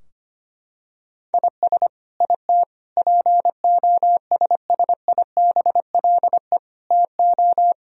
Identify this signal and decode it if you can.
Morse code, heard as a keyed tone, about 25 wpm, 700 Hz. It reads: IS IT POSSIBLE TO
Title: Sentences from Top 1000 Words - Review Part 4 - 25wpm